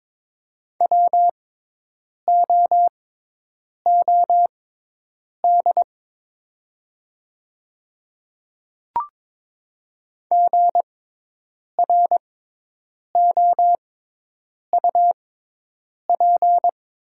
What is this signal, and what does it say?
Morse code, heard as a keyed tone, about 22 wpm, spaced as for 8 wpm, 700 Hz. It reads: WOOD GROUP
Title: 6x Character Spacing - Top 300 Words - 22wpm